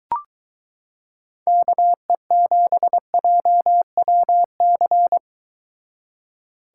Morse code, heard as a keyed tone, about 22 wpm, 700 Hz. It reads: KE7JWC